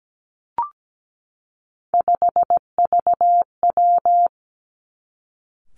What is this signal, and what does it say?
Morse code, heard as a keyed tone, about 17 wpm, 700 Hz. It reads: 5VW
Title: Three Challenging Letter-Number Combinations 17wpm